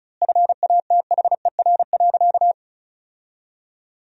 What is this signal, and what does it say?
Morse code, heard as a keyed tone, about 34 wpm, 700 Hz. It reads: FATHER.
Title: 4x Word-Spacing - Sentences from Top 500 Words - 35wpm